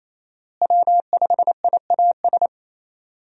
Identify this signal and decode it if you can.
Morse code, heard as a keyed tone, about 27 wpm, 700 Hz. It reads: W5IAS
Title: USA Call Signs - 5 Characters - Repeated 3x - 28wpm